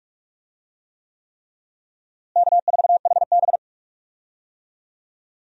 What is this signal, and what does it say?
Morse code, heard as a keyed tone, about 44 wpm, 700 Hz. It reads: K4HB
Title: USA Call Signs - 4 Characters - Repeated 3x - 45wpm